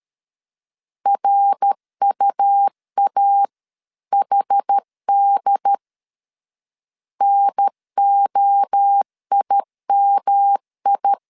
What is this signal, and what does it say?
Morse code, heard as a keyed tone, about 13 wpm, 800 Hz. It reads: RUAHD NOIMI
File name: CW / DET / 1 3 1500C S B 12wpm 2828